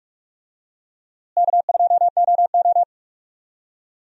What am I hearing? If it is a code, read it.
Morse code, heard as a keyed tone, about 44 wpm, 700 Hz. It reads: K2OO
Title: USA Call Signs - 4 Characters - Repeated 2x - 45wpm